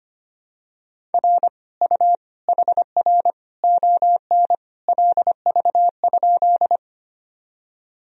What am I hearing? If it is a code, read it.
Morse code, heard as a keyed tone, about 25 wpm, 700 Hz. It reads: R U HR ON LV?
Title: Sentences from Top 400 Words and Abbreviations 25wpm